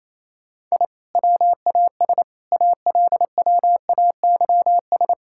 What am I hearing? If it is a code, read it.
Morse code, heard as a keyed tone, about 28 wpm, 700 Hz. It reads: I WAS ALWAYS
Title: Sentences from Top 1000 Words - Review Part 6 - 28wpm